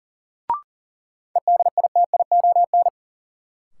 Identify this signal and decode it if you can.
Morse code, heard as a keyed tone, about 40 wpm, 700 Hz. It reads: EDITION
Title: Sets of 5 Words - New Word Drill - 40wpm